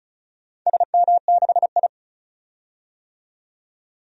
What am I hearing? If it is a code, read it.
Morse code, heard as a keyed tone, about 34 wpm, 700 Hz. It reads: SM6I
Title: DXCC-100 Call Signs - 4 Characters - 35wpm